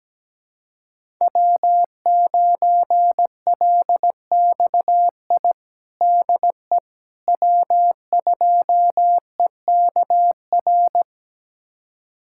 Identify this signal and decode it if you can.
Morse code, heard as a keyed tone, about 17 wpm, 700 Hz. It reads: W9LXI DE W2EKR
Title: USA Callsign DE Callsign - 5 Characters - 17wpm